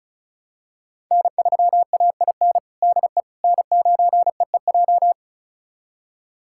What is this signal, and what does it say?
Morse code, heard as a keyed tone, about 36 wpm, 700 Hz. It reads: N3AIN DE N9EEJ
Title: USA Callsign DE Callsign 35wpm